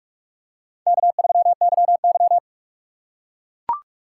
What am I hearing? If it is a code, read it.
Morse code, heard as a keyed tone, about 44 wpm, 700 Hz. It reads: K3YY
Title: USA Call Signs - 4 Characters - 6x Character Spacing - 45wpm